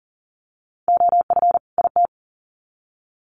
Morse code, heard as a keyed tone, about 40 wpm, 700 Hz. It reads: OF IT
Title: Sets of 2 Words 40wpm